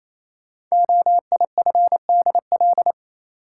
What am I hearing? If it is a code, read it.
Morse code, heard as a keyed tone, about 29 wpm, 700 Hz. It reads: OIFDL
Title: Code Groups - 5 Letters - 28wpm